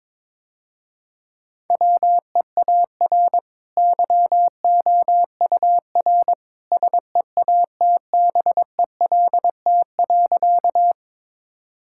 Morse code, heard as a keyed tone, about 22 wpm, 700 Hz. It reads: WEAR YOUR SEATBELT.
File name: practice / 22wpm / Easy Sentences with Hints - 22wpm